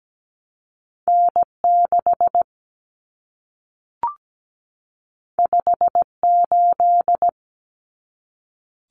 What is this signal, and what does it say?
Morse code, heard as a keyed tone, about 17 wpm, 700 Hz. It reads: N6 58
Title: Common Cut Numbers - 2-Digit 17wpm